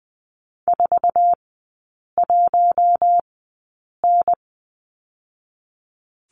Morse code, heard as a keyed tone, about 20 wpm, spaced as for 8 wpm, 690 Hz. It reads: 41N
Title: 093 Lesson 16e - Letter c - Three Characters 20wpm